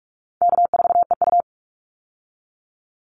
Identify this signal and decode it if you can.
Morse code, heard as a keyed tone, about 44 wpm, 700 Hz. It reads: K4EU
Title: Call Signs Repeated 3x - 45wpm